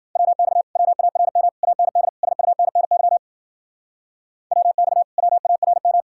Morse code, heard as a keyed tone, about 52 wpm, 700 Hz. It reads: WX WARM AND SUNNY WX WARM